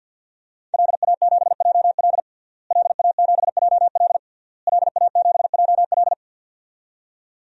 Morse code, heard as a keyed tone, about 46 wpm, 700 Hz. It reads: LA7JL LA7JL LA7JL
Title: DXCC-100 Call Signs - 5 Characters - Repeated 3x - 50wpm